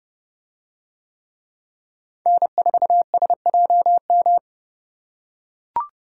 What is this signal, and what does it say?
Morse code, heard as a keyed tone, about 29 wpm, 700 Hz. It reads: N4SJM